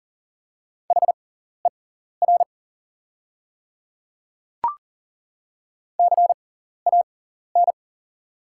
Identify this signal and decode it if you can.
Morse code, heard as a keyed tone, about 39 wpm, spaced as for 14 wpm, 700 Hz. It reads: HER CAN
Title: 6x Character Spacing - Top 100 Words - 40wpm